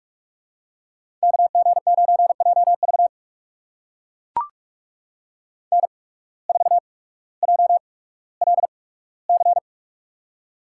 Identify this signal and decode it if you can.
Morse code, heard as a keyed tone, about 44 wpm, spaced as for 19 wpm, 700 Hz. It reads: KG9JV N4JLC
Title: USA Call Signs - 5 Characters - 8x Character Spacing - 45wpm